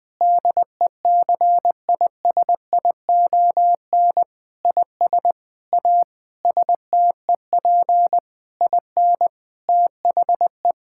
Morse code, heard as a keyed tone, about 21 wpm, 700 Hz. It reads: DECISION IS A STEP IN THE
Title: Sentences with 7-Word Phrases - Spoken Beginning - 20wpm